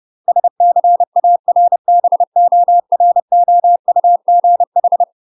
Morse code, heard as a keyed tone, about 30 wpm, 700 Hz. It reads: SCARBOROUGH
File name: UK MASTER QSOs TRAINING / 30WPM / UK MASTER QSOs 0005